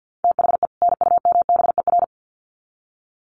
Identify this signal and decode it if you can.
Morse code, heard as a keyed tone, about 50 wpm, 700 Hz. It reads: THE NUMBER
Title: Sets of 2 Words 50wpm